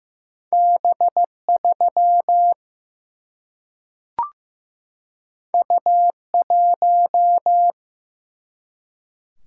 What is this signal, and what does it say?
Morse code, heard as a keyed tone, about 15 wpm, 700 Hz. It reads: B3 U1
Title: Single-Letter Single-Number 15wpm